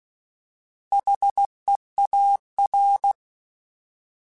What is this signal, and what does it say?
Morse code, heard as a keyed tone, about 16 wpm, 800 Hz. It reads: HEAR
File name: Alphabetical / Alphabetical mp3s / Hs @15